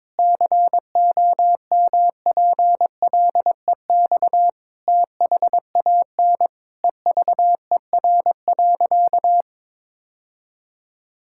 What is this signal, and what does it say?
Morse code, heard as a keyed tone, about 22 wpm, 700 Hz. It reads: COMPLEX THAN EVER.
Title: Sentences from Top 1000 Words - Review Part 6 - 22wpm